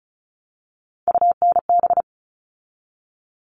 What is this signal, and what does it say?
Morse code, heard as a keyed tone, about 34 wpm, 700 Hz. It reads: UNB
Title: All Cut Numbers - 3-Digit 35wpm